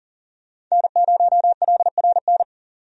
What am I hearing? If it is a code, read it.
Morse code, heard as a keyed tone, about 40 wpm, 700 Hz. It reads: N0LRN
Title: USA Call Signs - 5 Characters - 40wpm